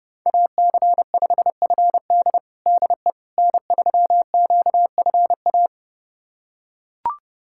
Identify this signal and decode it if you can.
Morse code, heard as a keyed tone, about 30 wpm, 700 Hz. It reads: AC5FD DE N3QFA
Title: USA Callsign DE Callsign - 5 Characters - 30wpm